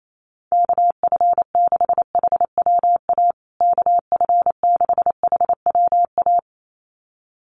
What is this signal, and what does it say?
Morse code, heard as a keyed tone, about 28 wpm, 700 Hz. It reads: KF6HWA KF6HWA
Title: USA Call Signs Repeated 2x 28wpm